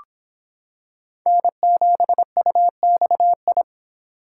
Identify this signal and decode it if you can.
Morse code, heard as a keyed tone, about 26 wpm, 700 Hz. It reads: N7UXI